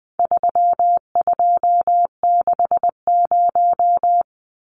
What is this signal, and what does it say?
Morse code, heard as a keyed tone, about 20 wpm, 700 Hz. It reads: 3260
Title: Numbers - 4-Digit 20wpm